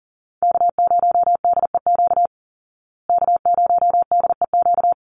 Morse code, heard as a keyed tone, about 40 wpm, 700 Hz. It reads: K0DEQ K0DEQ
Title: Call Signs Repeated 2x - 40wpm